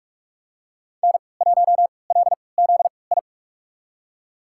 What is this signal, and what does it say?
Morse code, heard as a keyed tone, about 46 wpm, spaced as for 25 wpm, 700 Hz. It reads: N1LZI
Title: USA Call Signs - 5 Characters - 3x Character Spacing - 45wpm